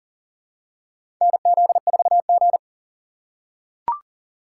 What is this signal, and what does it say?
Morse code, heard as a keyed tone, about 40 wpm, 700 Hz. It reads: NZ4G
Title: USA Call Signs - 4 Characters - 2x Character Spacing - 40wpm